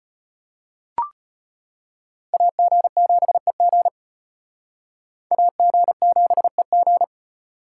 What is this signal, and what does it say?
Morse code, heard as a keyed tone, about 32 wpm, 700 Hz. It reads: AG7EG AG7EG